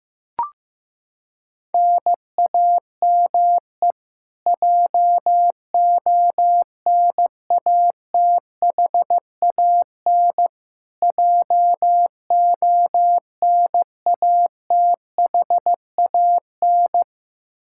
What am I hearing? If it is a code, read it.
Morse code, heard as a keyed tone, about 16 wpm, 700 Hz. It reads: NAME JONATHAN JONATHAN
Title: Practice QSO - Exchange 1 - Beginning - 15wpm